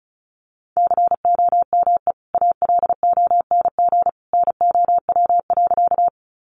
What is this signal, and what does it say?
Morse code, heard as a keyed tone, about 34 wpm, 700 Hz. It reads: COME ALONG NOW.
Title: Sentences from Top 700 Words 35wpm